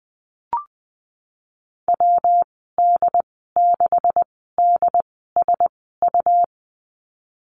Morse code, heard as a keyed tone, about 20 wpm, spaced as for 14 wpm, 700 Hz. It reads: WD6DSU
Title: 147 Lesson 24g - Letter y - Call Signs 20wpm